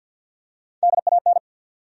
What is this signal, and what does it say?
Morse code, heard as a keyed tone, about 52 wpm, 700 Hz. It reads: DAN
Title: USA Common Names - Max 5 Letters - 50wpm